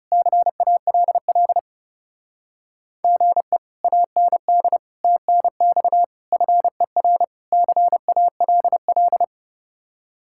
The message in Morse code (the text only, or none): CALL GE AND TNX FER CALL